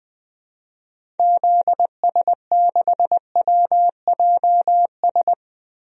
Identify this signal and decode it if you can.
Morse code, heard as a keyed tone, about 20 wpm, 700 Hz. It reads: ZS6WJS
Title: DXCC-100 Call Signs - 6 Characters - Repeated 3x - 20wpm